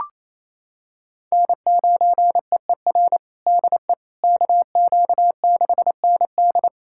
Morse code, heard as a keyed tone, about 28 wpm, 700 Hz. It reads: N9EER DE KQ6ND
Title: USA Callsign DE Callsign - 5 Characters - 28wpm